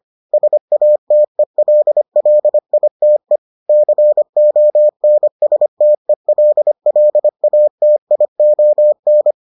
Morse code, heard as a keyed tone, about 25 wpm, 600 Hz. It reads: SATELLITE CONSTELLATION